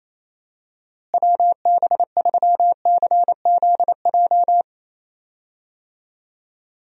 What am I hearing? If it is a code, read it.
Morse code, heard as a keyed tone, about 28 wpm, 700 Hz. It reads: WB3CZJ
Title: USA Call Signs - 6 Characters - Repeated 2x - 28wpm